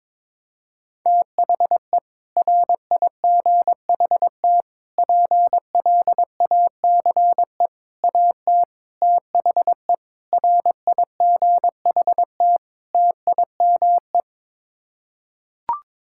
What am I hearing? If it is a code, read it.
Morse code, heard as a keyed tone, about 22 wpm, 700 Hz. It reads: THE RIGHT PLACE AT THE RIGHT TIME
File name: Sets of 7 Words - 22wpm